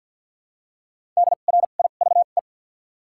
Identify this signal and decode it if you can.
Morse code, heard as a keyed tone, about 53 wpm, spaced as for 37 wpm, 700 Hz. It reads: DRIVE
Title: 2x Character Spacing - Top 500 Words - 50wpm